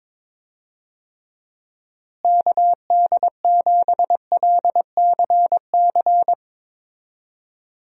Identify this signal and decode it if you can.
Morse code, heard as a keyed tone, about 22 wpm, 700 Hz. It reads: KD7LCC
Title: USA Call Signs - 6 Characters - Repeated 2x - 22wpm